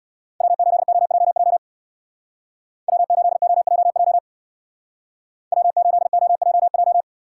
Morse code, heard as a keyed tone, about 60 wpm, 700 Hz. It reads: W8OJJ W8OJJ W8OJJ
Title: Speed Racing - USA 1x3 Call Signs 50wpm